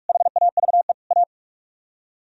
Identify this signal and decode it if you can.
Morse code, heard as a keyed tone, about 58 wpm, 700 Hz. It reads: HAVE A